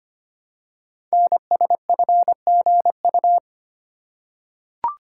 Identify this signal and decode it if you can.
Morse code, heard as a keyed tone, about 25 wpm, 700 Hz. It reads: NSFGU
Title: Code Groups - 5 Letters - 25wpm